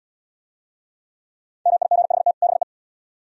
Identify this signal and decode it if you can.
Morse code, heard as a keyed tone, about 78 wpm, 700 Hz. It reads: MIGHT BE